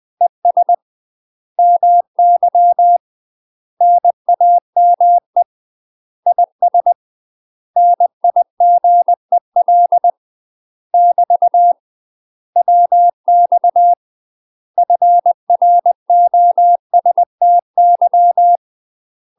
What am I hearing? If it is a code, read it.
Morse code, heard as a keyed tone, about 20 wpm, 700 Hz. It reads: ES MY NAME IS NIGEL <BT> WX FROSTY
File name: UK MASTER QSOs TRAINING / 20WPM / UK MASTER QSOs 0058